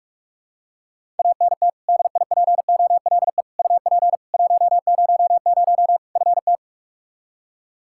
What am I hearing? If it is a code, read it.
Morse code, heard as a keyed tone, about 46 wpm, 700 Hz. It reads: ANT DIPOLE UP 100 FT
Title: Practice QSO - Exchange 2 - Info - 45wpm